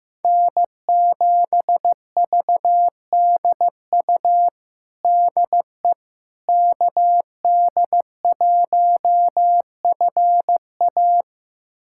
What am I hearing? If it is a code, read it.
Morse code, heard as a keyed tone, about 15 wpm, 700 Hz. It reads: N7VDU DE KD1FA
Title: USA Callsign DE Callsign - 5 Characters - 15wpm